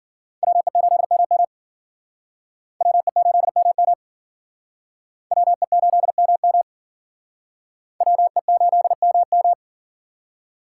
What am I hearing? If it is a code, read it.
Morse code, heard as a keyed tone, about 51 wpm, 700 Hz. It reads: WE8MM WE8MM WE8MM WE8MM